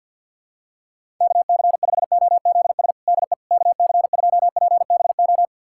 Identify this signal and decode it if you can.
Morse code, heard as a keyed tone, about 50 wpm, 700 Hz. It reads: KC5OZS DE KC2PBO
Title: USA Callsign DE Callsign 50wpm